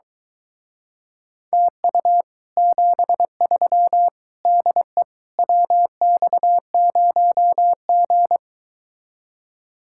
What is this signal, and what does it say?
Morse code, heard as a keyed tone, about 22 wpm, 700 Hz. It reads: TU 73 DE WX0G